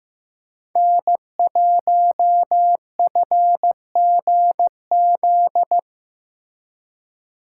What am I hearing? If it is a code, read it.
Morse code, heard as a keyed tone, about 16 wpm, 700 Hz. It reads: N1FGZ
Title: Speed Racing - USA Call Signs 15wpm